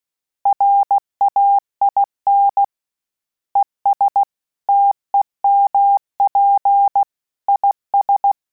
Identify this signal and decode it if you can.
Morse code, heard as a keyed tone, about 16 wpm, 800 Hz. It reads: RAIN ES TEMP IS
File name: Telegraphy-group / QSO CW 15 PPM